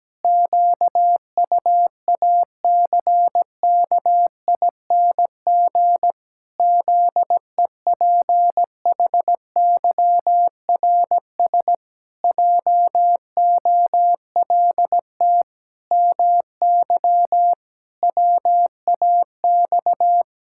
Morse code, heard as a keyed tone, about 17 wpm, 700 Hz. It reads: QUACKING ZEPHYRS JOLT MY WAX
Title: Pangram Challenge v2 - 17wpm